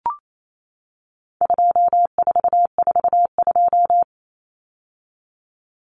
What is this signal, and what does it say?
Morse code, heard as a keyed tone, about 28 wpm, 700 Hz. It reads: 2442
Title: Numbers - 4-Digit 28wpm